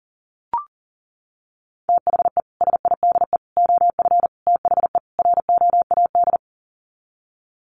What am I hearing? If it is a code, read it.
Morse code, heard as a keyed tone, about 37 wpm, 700 Hz. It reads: THE SIDE OF THE ROAD
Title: Sets of 5 Words 40wpm